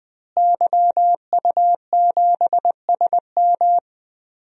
Morse code, heard as a keyed tone, about 20 wpm, 700 Hz. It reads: YU7SM
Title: DXCC-100 Call Signs - 5 Characters - Repeated 3x - 20wpm